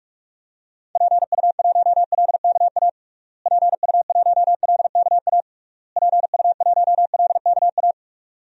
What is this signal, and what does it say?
Morse code, heard as a keyed tone, about 46 wpm, 700 Hz. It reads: PU1LKA PU1LKA PU1LKA
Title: DXCC-100 Call Signs - 6 Characters - Repeated 3x - 45wpm